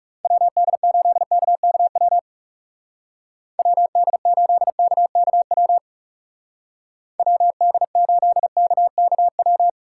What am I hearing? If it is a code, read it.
Morse code, heard as a keyed tone, about 36 wpm, 700 Hz. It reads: WD8KKW WD8KKW WD8KKW
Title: Speed Racing - USA 2x3 Call Signs 30wpm